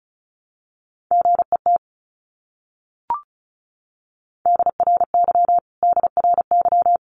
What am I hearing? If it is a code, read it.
Morse code, heard as a keyed tone, about 34 wpm, 700 Hz. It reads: GET DRY DRY
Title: Top 500 Words Repeated 35wpm